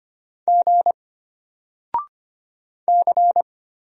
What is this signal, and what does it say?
Morse code, heard as a keyed tone, about 25 wpm, 700 Hz. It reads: G C
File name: Single-Letter-Number-25wpm